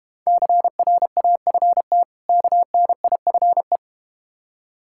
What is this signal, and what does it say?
Morse code, heard as a keyed tone, about 30 wpm, 700 Hz. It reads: CRAFT KNIFE
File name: practice / 25wpm / Difficult Words - Speed-Racing with Hints - 25wpm